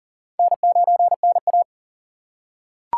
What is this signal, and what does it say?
Morse code, heard as a keyed tone, about 40 wpm, 700 Hz. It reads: N9NA
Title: USA Call Signs - 4 Characters - 2x Character Spacing - 40wpm